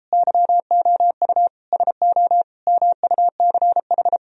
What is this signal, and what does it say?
Morse code, heard as a keyed tone, about 33 wpm, 700 Hz. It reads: YOU SO MUCH